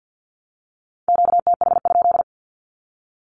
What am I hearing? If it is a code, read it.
Morse code, heard as a keyed tone, about 51 wpm, 700 Hz. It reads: QTH?
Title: Q-Codes 50wpm